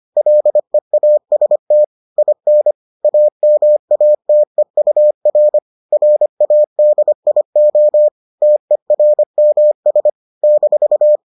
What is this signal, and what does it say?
Morse code, heard as a keyed tone, about 25 wpm, 600 Hz. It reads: LEAST IN AMATEUR RADIO TERMS -